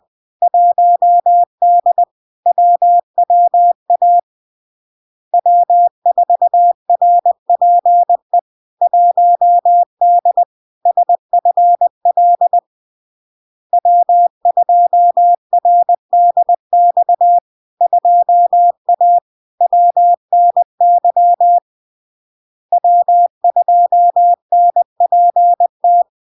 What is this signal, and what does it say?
Morse code, heard as a keyed tone, about 20 wpm, 700 Hz. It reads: 1D WWA W4RPE 1D SFL W2RDX 2A WNY W2NPT
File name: LARK-FD-2022-Log-in-CW-20wpm-700-Hz